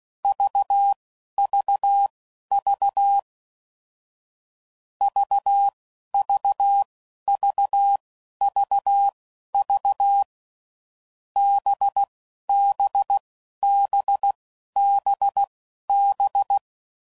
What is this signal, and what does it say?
Morse code, heard as a keyed tone, about 16 wpm, spaced as for 11 wpm, 800 Hz. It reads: VVV VVVVV BBBBB